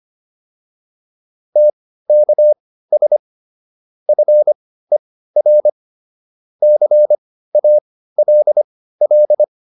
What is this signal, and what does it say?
Morse code, heard as a keyed tone, about 25 wpm, spaced as for 15 wpm, 600 Hz. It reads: TKS FER CALL